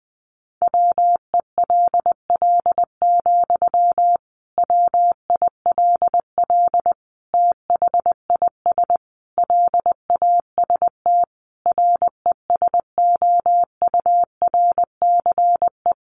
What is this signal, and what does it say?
Morse code, heard as a keyed tone, about 20 wpm, 700 Hz. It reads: WELL, WILL THIS LAST RESOURCE
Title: Sentences from Top 1300 Words 20wpm